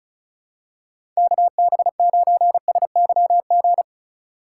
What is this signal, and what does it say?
Morse code, heard as a keyed tone, about 36 wpm, 700 Hz. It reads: KB9SYG
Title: USA Call Signs - 2x Character Spacing - 35wpm